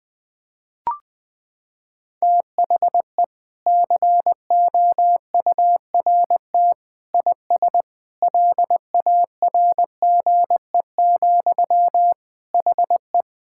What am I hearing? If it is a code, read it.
Morse code, heard as a keyed tone, about 20 wpm, 700 Hz. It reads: THE COURT IS LARGE, HE
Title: Sentences from Top 1600 Words 20wpm